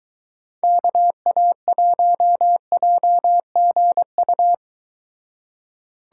Morse code, heard as a keyed tone, about 23 wpm, 700 Hz. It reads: KA1JGU